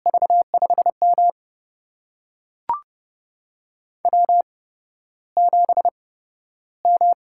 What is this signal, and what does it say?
Morse code, heard as a keyed tone, about 30 wpm, spaced as for 13 wpm, 700 Hz. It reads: V5M W7M
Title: USA Call Signs - 4 Characters - 8x Character Spacing - 30wpm